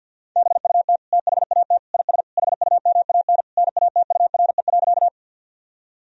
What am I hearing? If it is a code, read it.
Morse code, heard as a keyed tone, about 46 wpm, 700 Hz. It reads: BUT THAT IS HUMAN NATURE.
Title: Sentences from Top 1000 Words - Review Part 3 - 50wpm